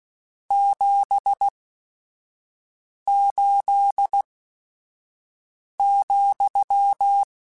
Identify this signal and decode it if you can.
Morse code, heard as a keyed tone, about 16 wpm, spaced as for 5 wpm, 800 Hz. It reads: 78,
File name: Your RSTs @5